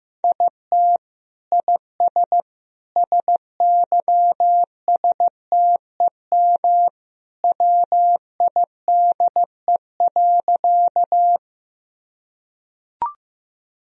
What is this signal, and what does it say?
Morse code, heard as a keyed tone, about 15 wpm, 700 Hz. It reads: IT IS SYSTEM WIDE.
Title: Sentences from Top 1000 Words - Review Part 3 - 15wpm